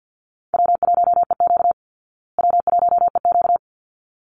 Morse code, heard as a keyed tone, about 49 wpm, 700 Hz. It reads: W1EQ W1EQ
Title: Easy Call Signs Repeated 2x - 50wpm